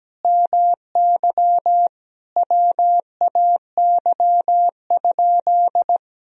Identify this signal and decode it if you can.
Morse code, heard as a keyed tone, about 17 wpm, 700 Hz. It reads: MY WAY?
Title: Sentences from Top 200 Words and Abbreviations 17wpm